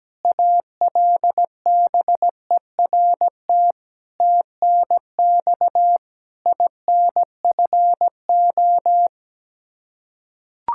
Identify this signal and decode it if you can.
Morse code, heard as a keyed tone, about 17 wpm, 700 Hz. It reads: ALBERT TNX INFO
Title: Practice QSO - Exchange 3 - More Info - 17wpm